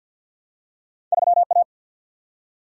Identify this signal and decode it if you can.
Morse code, heard as a keyed tone, about 51 wpm, 700 Hz. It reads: 3 A
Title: Single-Letter Single-Number 50wpm